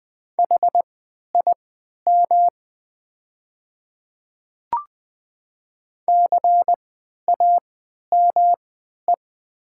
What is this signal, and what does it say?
Morse code, heard as a keyed tone, about 20 wpm, spaced as for 11 wpm, 700 Hz. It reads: HIM CAME
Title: 3x Character Spacing - Top 300 Words - 20wpm